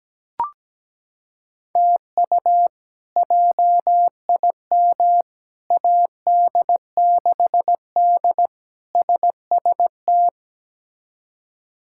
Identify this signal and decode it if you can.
Morse code, heard as a keyed tone, about 17 wpm, 700 Hz. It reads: TU JIM AD6D SST
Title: K1USN SST QSO Practice - 17wpm